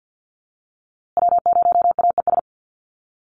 Morse code, heard as a keyed tone, about 49 wpm, 700 Hz. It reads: W0AES
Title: USA Call Signs Repeated 3x 50wpm